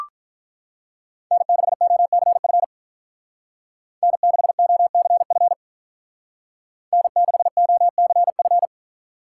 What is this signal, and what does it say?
Morse code, heard as a keyed tone, about 44 wpm, 700 Hz. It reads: N6OCF N6OCF N6OCF